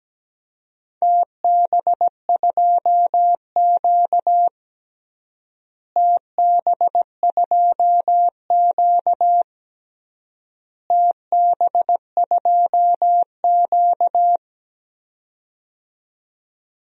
Morse code, heard as a keyed tone, about 17 wpm, 700 Hz. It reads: TB2Q TB2Q TB2Q
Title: DXCC-100 Call Signs - 4 Characters - Repeated 3x - 17wpm